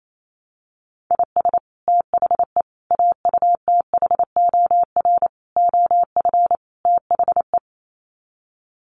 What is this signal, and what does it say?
Morse code, heard as a keyed tone, about 28 wpm, 700 Hz. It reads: IS THE AUTHOR OF THE
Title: Sets of 5 Words Encore 28wpm